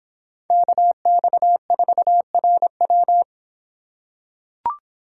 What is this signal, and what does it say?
Morse code, heard as a keyed tone, about 25 wpm, 700 Hz. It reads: KX4RW